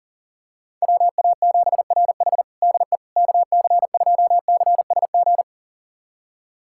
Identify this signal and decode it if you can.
Morse code, heard as a keyed tone, about 40 wpm, 700 Hz. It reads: WA7RH DE KC2CSG
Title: USA Callsign DE Callsign 40wpm